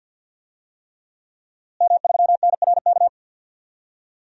Morse code, heard as a keyed tone, about 51 wpm, 700 Hz. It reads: M3NRK
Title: DXCC-100 Call Signs - 5 Characters - Repeated 3x - 50wpm